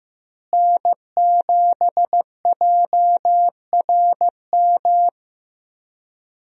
Morse code, heard as a keyed tone, about 15 wpm, 700 Hz. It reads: N7JRM
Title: USA Call Signs - 5 Characters - Repeated 2x - 15wpm